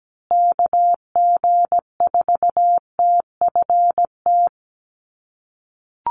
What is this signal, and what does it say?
Morse code, heard as a keyed tone, about 17 wpm, 700 Hz. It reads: KG4TFT
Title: USA Call Signs Repeated 2x 17wpm